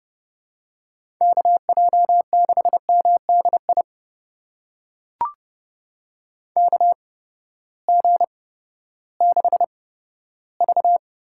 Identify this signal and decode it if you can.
Morse code, heard as a keyed tone, about 31 wpm, spaced as for 15 wpm, 700 Hz. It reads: KJ6MDI KG6V
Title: USA Call Signs - 6 Characters - 8x Character Spacing - 30wpm